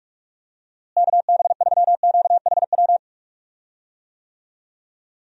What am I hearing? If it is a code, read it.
Morse code, heard as a keyed tone, about 44 wpm, 700 Hz. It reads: KB3QHW
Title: USA Call Signs - 6 Characters - Repeated 2x - 45wpm